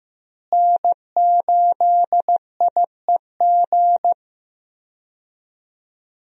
Morse code, heard as a keyed tone, about 16 wpm, 700 Hz. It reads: N8IEG
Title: USA Call Signs - 5 Characters - Repeated 2x - 15wpm